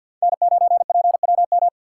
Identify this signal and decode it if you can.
Morse code, heard as a keyed tone, about 53 wpm, 700 Hz. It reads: N9PWM
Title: USA Call Signs - 5 Characters - 2x Character Spacing - 50wpm